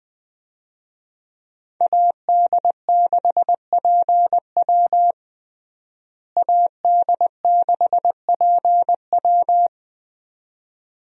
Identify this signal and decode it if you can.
Morse code, heard as a keyed tone, about 20 wpm, 700 Hz. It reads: AD6PW AD6PW
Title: USA Call Signs - 5 Characters - Repeated 2x - 20wpm